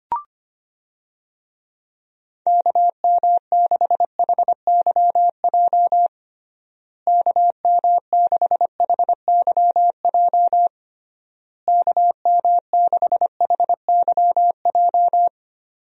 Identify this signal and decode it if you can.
Morse code, heard as a keyed tone, about 25 wpm, 700 Hz. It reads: KM6HYJ KM6HYJ KM6HYJ
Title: USA Call Signs - 6 Characters - Repeated 3x - 25wpm